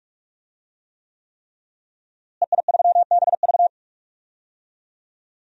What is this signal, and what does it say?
Morse code, heard as a keyed tone, about 41 wpm, 700 Hz. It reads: EI3BV
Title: DXCC-100 Call Signs - 5 Characters - Repeated 3x - 45wpm